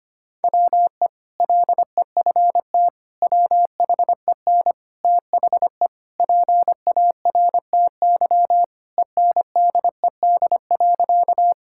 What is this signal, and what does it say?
Morse code, heard as a keyed tone, about 25 wpm, 700 Hz. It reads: WE LEFT WHEN THE PARTY ENDED.
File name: practice / 25wpm / Sentences with 2-Word Phrases - 25wpm